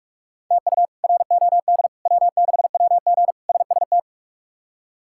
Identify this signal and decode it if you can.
Morse code, heard as a keyed tone, about 41 wpm, 700 Hz. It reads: TU ROD W6WG SST
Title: K1USN SST QSO Practice - 45wpm